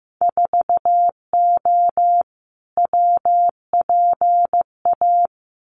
Morse code, heard as a keyed tone, about 15 wpm, 700 Hz. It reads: 4O WPA
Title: Winter Field Day Exchanges 15wpm